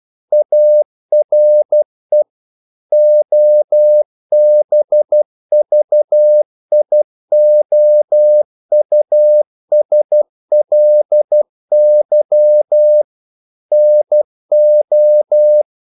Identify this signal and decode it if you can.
Morse code, heard as a keyed tone, about 12 wpm, 600 Hz. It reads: ARE OBVIOUSLY NO
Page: English quotations in CW (telegraphy - morse)